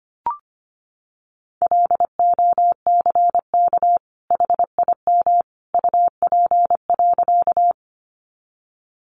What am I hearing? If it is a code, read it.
Morse code, heard as a keyed tone, about 25 wpm, 700 Hz. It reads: LOCK HIM UP.
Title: Sentences from Top 1300 Words 25wpm